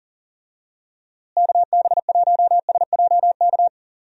Morse code, heard as a keyed tone, about 39 wpm, 700 Hz. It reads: KB1SJK
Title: USA Call Signs - 6 Characters - Repeated 2x - 40wpm